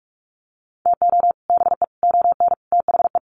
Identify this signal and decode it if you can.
Morse code, heard as a keyed tone, about 44 wpm, 700 Hz. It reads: TO BE ON THE
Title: Sets of 4 Words 45wpm